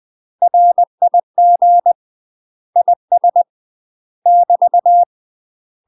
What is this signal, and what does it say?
Morse code, heard as a keyed tone, about 20 wpm, 700 Hz. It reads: RIG IS <BT>
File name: UK MASTER QSOs TRAINING / 20WPM / UK MASTER QSOs 0055